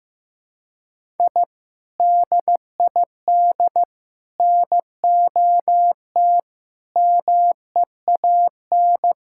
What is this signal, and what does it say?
Morse code, heard as a keyed tone, about 16 wpm, 700 Hz. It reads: I DID NOT MEAN
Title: Sentences from Top 1000 Words - Review Part 6 - 15wpm